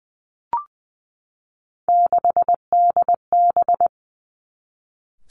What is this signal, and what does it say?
Morse code, heard as a keyed tone, about 20 wpm, 700 Hz. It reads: 6DB
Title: Three Challenging Letter-Number Combinations 20wpm